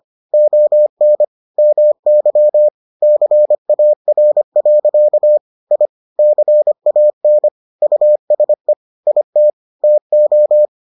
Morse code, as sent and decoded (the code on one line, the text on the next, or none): --- -. / -- -.-- / -.-. .- .-. .-.-.- / .. / -.-. .- -. / ..- ... / .. - / - ---
ON MY CAR. I CAN USE IT TO